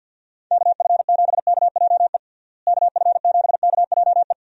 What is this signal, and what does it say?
Morse code, heard as a keyed tone, about 46 wpm, 700 Hz. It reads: KF7KJE KF7KJE
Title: USA Call Signs - 6 Characters - Repeated 2x - 50wpm